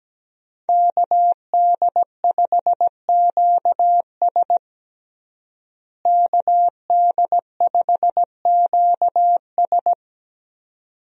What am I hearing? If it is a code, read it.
Morse code, heard as a keyed tone, about 17 wpm, 700 Hz. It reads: KD5QS KD5QS
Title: USA Call Signs - 5 Characters - Repeated 2x - 17wpm